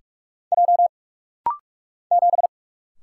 Morse code, heard as a keyed tone, about 44 wpm, 700 Hz. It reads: J 7
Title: Single Letter-Number - Rapid-Fire 45wpm